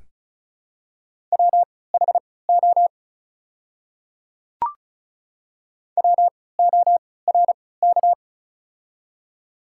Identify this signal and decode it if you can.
Morse code, heard as a keyed tone, about 34 wpm, spaced as for 20 wpm, 700 Hz. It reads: WHO WORK
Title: 3x Character Spacing - Top 500 Words - 35wpm